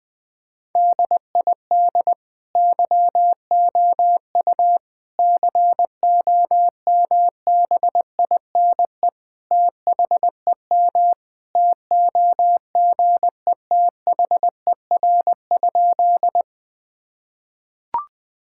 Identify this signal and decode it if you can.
Morse code, heard as a keyed tone, about 20 wpm, 700 Hz. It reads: DID YOU COMBINE THEM TOGETHER?
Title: Sentences from Top 1600 Words 20wpm